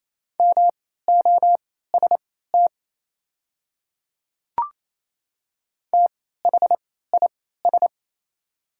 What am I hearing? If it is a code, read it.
Morse code, heard as a keyed tone, about 27 wpm, spaced as for 16 wpm, 690 Hz. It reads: MOST THIS
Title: 3x Character Spacing - Top 100 Words - 28wpm